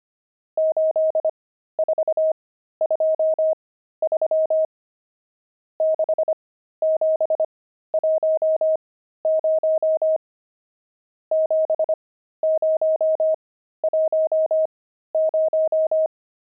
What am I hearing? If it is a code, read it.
Morse code, heard as a keyed tone, about 25 wpm, spaced as for 13 wpm, 630 Hz. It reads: 8423 6710 7010
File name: pangram-fs-13-cs-25-nums16